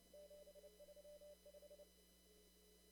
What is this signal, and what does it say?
Morse code, heard as a keyed tone, about 31 wpm, 600 Hz. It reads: 735
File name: CW / kt5j wpx cw-op-k5pi